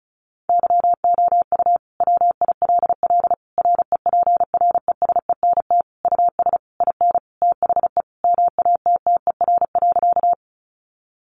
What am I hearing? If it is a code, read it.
Morse code, heard as a keyed tone, about 34 wpm, 700 Hz. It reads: YOU WILL REPRESENT US IN THE MATTER.
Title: Sentences from Top 1200 Words 35wpm